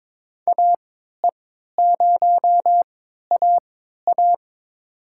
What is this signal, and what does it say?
Morse code, heard as a keyed tone, about 22 wpm, spaced as for 12 wpm, 700 Hz. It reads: AE0AA
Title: USA Call Signs - 5 Characters - 3x Character Spacing - 22wpm